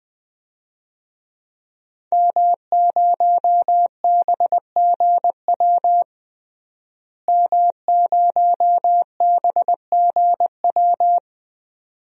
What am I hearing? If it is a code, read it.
Morse code, heard as a keyed tone, about 21 wpm, 700 Hz. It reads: M0BGW M0BGW
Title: DXCC-100 Call Signs - 5 Characters - Repeated 2x - 20wpm